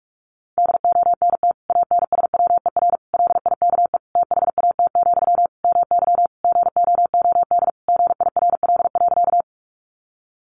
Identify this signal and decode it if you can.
Morse code, heard as a keyed tone, about 46 wpm, 700 Hz. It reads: DONT ANSWER LIKE THAT, MY GOOD GIRL.
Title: Sentences from Top 400 Words 45wpm